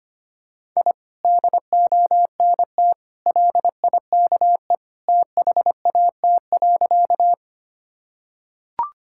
Morse code, heard as a keyed tone, about 25 wpm, 700 Hz. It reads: I DONT LIKE THAT.
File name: practice / 25wpm / Sentences with 2-Word Phrases - 25wpm